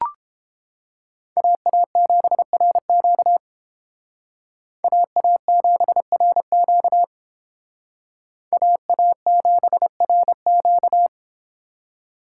Morse code, heard as a keyed tone, about 31 wpm, 700 Hz. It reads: AA7RQ AA7RQ AA7RQ